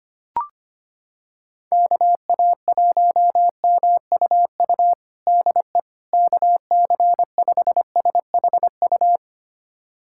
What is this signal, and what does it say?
Morse code, heard as a keyed tone, about 25 wpm, 700 Hz. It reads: KA1MUU DE KC5SHU
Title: USA Callsign DE Callsign - 6 Characters - 25wpm